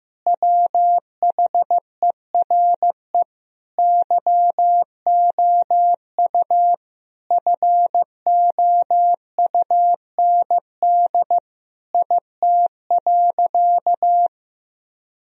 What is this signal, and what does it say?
Morse code, heard as a keyed tone, about 15 wpm, 700 Hz. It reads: WHERE YOU FOUND IT.
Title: Sentences from Top 1000 Words - Review Part 4 - 15wpm